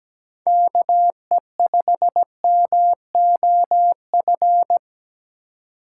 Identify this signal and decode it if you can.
Morse code, heard as a keyed tone, about 20 wpm, 700 Hz. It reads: KE5MOF
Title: Speed Racing - USA Call Signs 17wpm